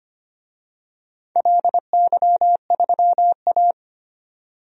Morse code, heard as a keyed tone, about 25 wpm, 700 Hz. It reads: LY3A
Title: DXCC-100 Call Signs - 4 Characters - Repeated 2x - 25wpm